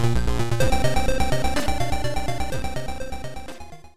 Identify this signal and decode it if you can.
There is no code.